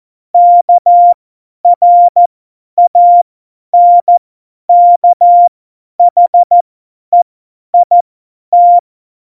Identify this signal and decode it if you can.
Morse code, heard as a keyed tone, about 14 wpm, spaced as for 10 wpm, 700 Hz. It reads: KRANKHEIT
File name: Morsekurs / Witze / Witz 06. Oktober 2020, 50 F70